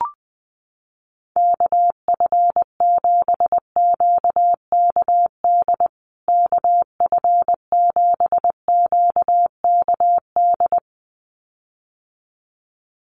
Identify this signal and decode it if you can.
Morse code, heard as a keyed tone, about 20 wpm, 700 Hz. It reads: KF7QKD KF7QKD
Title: USA Call Signs Repeated 2x 20wpm